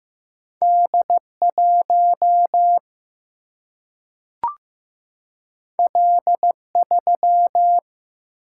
Morse code, heard as a keyed tone, about 15 wpm, 700 Hz. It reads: D 1 L 3
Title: Single-Letter Single-Number 15wpm